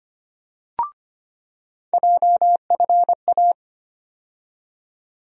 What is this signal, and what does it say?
Morse code, heard as a keyed tone, about 25 wpm, 700 Hz. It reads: JFA
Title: USA Call Sign Postfixes - 3 Letters 25wpm